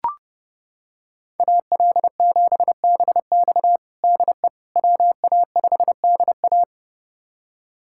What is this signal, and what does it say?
Morse code, heard as a keyed tone, about 32 wpm, 700 Hz. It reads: AL7BX DE WA5DA
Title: USA Callsign DE Callsign - 5 Characters - 30wpm